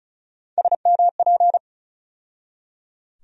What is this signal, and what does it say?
Morse code, heard as a keyed tone, about 34 wpm, 700 Hz. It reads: SMP
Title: USA Call Sign Postfixes - 3 Letters 35wpm